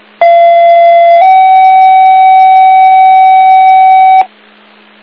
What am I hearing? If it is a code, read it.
Two-tone page, heard as a one-second tone, then a three-second tone, about 680 then 760 Hz, with noise. It is heard from 0.2 to 4.2 s.
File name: Inlet EMS